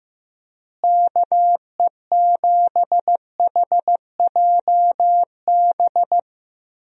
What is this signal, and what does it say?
Morse code, heard as a keyed tone, about 15 wpm, 700 Hz. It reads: KE7HJB
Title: USA Call Signs - 6 Characters - Repeated 2x - 15wpm